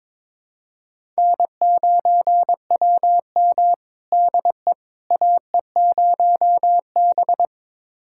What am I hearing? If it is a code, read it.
Morse code, heard as a keyed tone, about 22 wpm, 700 Hz. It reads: N9WM DE AE0B
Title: USA Callsign DE Callsign - 4 Characters - 22wpm